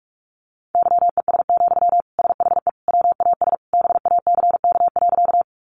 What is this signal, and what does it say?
Morse code, heard as a keyed tone, about 46 wpm, 700 Hz. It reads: YES, SHE WAS BACK.
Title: Sentences from Top 500 Words 45wpm